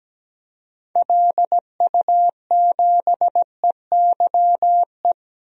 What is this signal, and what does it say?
Morse code, heard as a keyed tone, about 17 wpm, 700 Hz. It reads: LU7EYE
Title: DXCC-100 Call Signs - 6 Characters - 17wpm